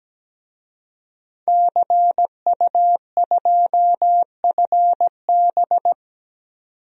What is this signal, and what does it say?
Morse code, heard as a keyed tone, about 17 wpm, 700 Hz. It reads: CU2FB
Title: DXCC-100 Call Signs - 5 Characters - 17wpm